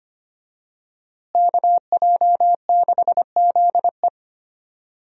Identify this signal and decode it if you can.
Morse code, heard as a keyed tone, about 25 wpm, 700 Hz. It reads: KJ6ZE
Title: USA Call Signs - 5 Characters - Repeated 3x - 25wpm